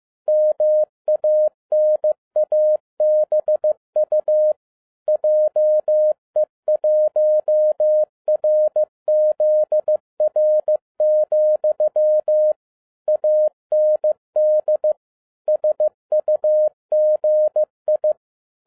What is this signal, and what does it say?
Morse code, heard as a keyed tone, about 15 wpm, 610 Hz. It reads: MANABU JE1RZR, AND SUGI
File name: 2012-01-mp3 / n17-SUMMARY OF FEA NET IN 2011-15wpm